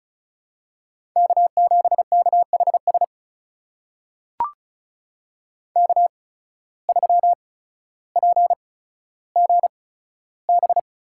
Morse code, heard as a keyed tone, about 36 wpm, spaced as for 15 wpm, 700 Hz. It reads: K7KHS K3PGB
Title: USA Call Signs - 5 Characters - 8x Character Spacing - 35wpm